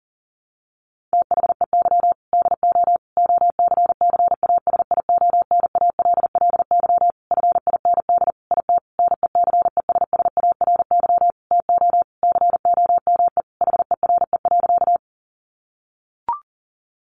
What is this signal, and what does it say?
Morse code, heard as a keyed tone, about 40 wpm, 700 Hz. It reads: THEY DO OCCASIONALLY FIND IT NECESSARY TO COME HERE.
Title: Sentences from Top 1300 Words 40wpm